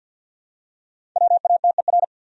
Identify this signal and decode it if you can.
Morse code, heard as a keyed tone, about 52 wpm, 700 Hz. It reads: WATER
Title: Top 100 Words 50wpm